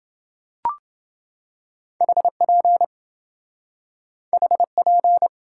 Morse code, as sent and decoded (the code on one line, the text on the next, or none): .... .--. / .... .--.
HP HP